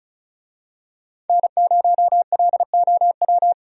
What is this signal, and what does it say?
Morse code, heard as a keyed tone, about 34 wpm, 700 Hz. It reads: N0LOW
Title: USA Call Signs - 5 Characters - 35wpm